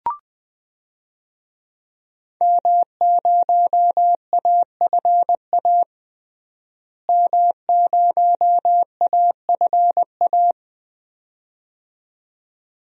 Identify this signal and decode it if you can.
Morse code, heard as a keyed tone, about 20 wpm, 700 Hz. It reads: M0AFA M0AFA
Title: DXCC-100 Call Signs - 5 Characters - Repeated 2x - 20wpm